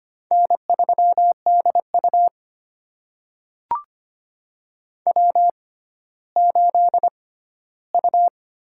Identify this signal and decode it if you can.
Morse code, heard as a keyed tone, about 25 wpm, spaced as for 14 wpm, 700 Hz. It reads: N3DU W8U
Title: USA Call Signs - 4 Characters - 6x Character Spacing - 25wpm